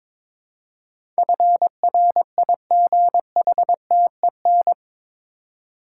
Morse code, heard as a keyed tone, about 22 wpm, 700 Hz. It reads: FRIGHTEN
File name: practice / 22wpm / Eight Letter Words - 22wpm